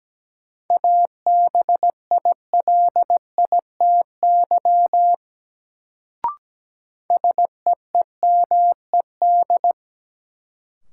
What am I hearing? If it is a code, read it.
Morse code, heard as a keyed tone, about 17 wpm, 700 Hz. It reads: ABILITY SEEMED
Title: Sets of 3 Words - New Word Drill - 17wpm